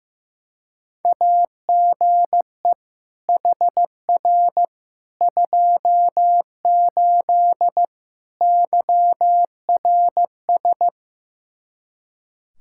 Morse code, heard as a keyed tone, about 16 wpm, 700 Hz. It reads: AGE HR 28 YRS
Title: Practice QSO - Exchange 3 - More Info - 15wpm